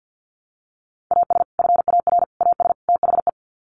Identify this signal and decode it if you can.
Morse code, heard as a keyed tone, about 50 wpm, 700 Hz. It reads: AS FAR AS THE
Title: Sets of 4 Words 50wpm